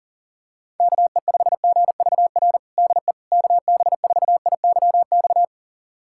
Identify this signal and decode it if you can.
Morse code, heard as a keyed tone, about 40 wpm, 700 Hz. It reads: KE5GVR DE KB4IYX
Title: USA Callsign DE Callsign 40wpm